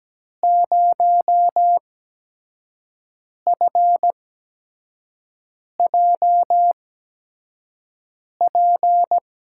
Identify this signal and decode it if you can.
Morse code, heard as a keyed tone, about 17 wpm, spaced as for 5 wpm, 700 Hz. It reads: 0FJP
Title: USA Call Signs - 6 Characters - 8x Character Spacing - 17wpm